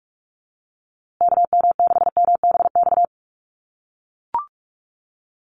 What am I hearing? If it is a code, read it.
Morse code, heard as a keyed tone, about 44 wpm, 700 Hz. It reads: KM6MBX
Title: USA Call Signs Repeated 3x 45wpm